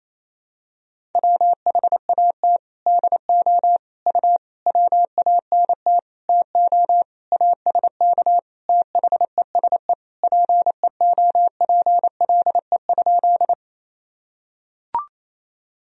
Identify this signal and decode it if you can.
Morse code, heard as a keyed tone, about 28 wpm, 700 Hz. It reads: WHAT DO U WANT TO ASK THESE PEOPLE?
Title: Sentences from Top 200 Words and Abbreviations 28wpm